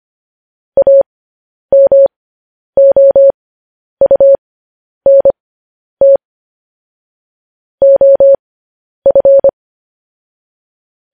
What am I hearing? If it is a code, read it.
Morse code, heard as a keyed tone, about 25 wpm, spaced as for 10 wpm, 550 Hz. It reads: AMOUNT OF
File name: podcast / News headlines-1x-25x10